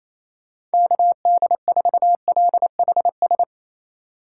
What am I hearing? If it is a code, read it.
Morse code, heard as a keyed tone, about 36 wpm, 700 Hz. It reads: KD4LHS